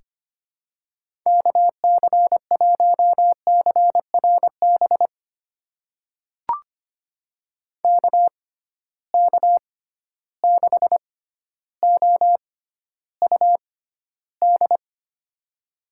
Morse code, heard as a keyed tone, about 25 wpm, spaced as for 13 wpm, 700 Hz. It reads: KC1CRB KK6OUD
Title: USA Call Signs - 6 Characters - 6x Character Spacing - 25wpm